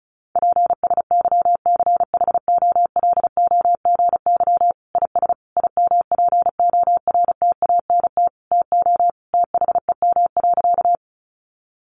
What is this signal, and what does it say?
Morse code, heard as a keyed tone, about 34 wpm, 700 Hz. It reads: PSYCHOLOGY IS IMPORTANT TO THEM.
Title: Sentences from Top 1400 Words 35wpm